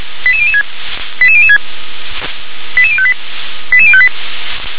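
Five-tone selective call (ZVEI), heard as quick runs of five tones, tones of about 70 ms. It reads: 88006 88006 88058 88058
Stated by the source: ZVEI five-tone selective call 88154, received on SDR.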